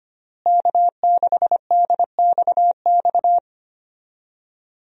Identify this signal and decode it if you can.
Morse code, heard as a keyed tone, about 25 wpm, 700 Hz. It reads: K6DXX
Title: USA Call Signs - 5 Characters - Repeated 3x - 25wpm